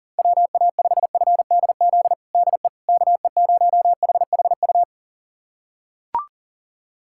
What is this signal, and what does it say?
Morse code, heard as a keyed tone, about 40 wpm, 700 Hz. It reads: WA5FDZ DE KE0HHU
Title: USA Callsign DE Callsign 40wpm